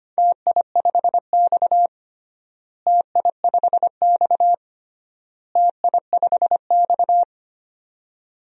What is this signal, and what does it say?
Morse code, heard as a keyed tone, about 25 wpm, 700 Hz. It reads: TI5X TI5X TI5X
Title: DXCC-100 Call Signs - 4 Characters - Repeated 3x - 25wpm